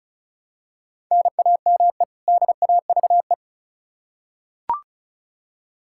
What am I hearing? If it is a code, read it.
Morse code, heard as a keyed tone, about 36 wpm, 700 Hz. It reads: NAME DAVE
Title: Practice QSO - Exchange 1 - Beginning - 35wpm